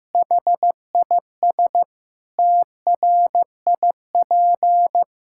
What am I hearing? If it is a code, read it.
Morse code, heard as a keyed tone, about 15 wpm, 700 Hz. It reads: HIS TRIP
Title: Pangram Challenge v2 - 15wpm